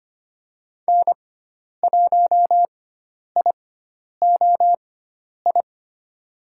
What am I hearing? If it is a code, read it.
Morse code, heard as a keyed tone, about 25 wpm, spaced as for 10 wpm, 700 Hz. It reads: N1IOI
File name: Lesson 16g - Letter c - Call Signs 25wpm